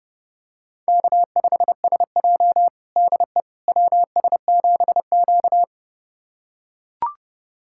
Morse code, heard as a keyed tone, about 30 wpm, 700 Hz. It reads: K5SJ DE WS7Q
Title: USA Callsign DE Callsign - 4 Characters - 30wpm